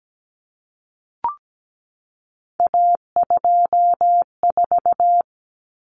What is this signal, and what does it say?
Morse code, heard as a keyed tone, about 17 wpm, 700 Hz. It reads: A24
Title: Common Cut Numbers - 3-Digit 17wpm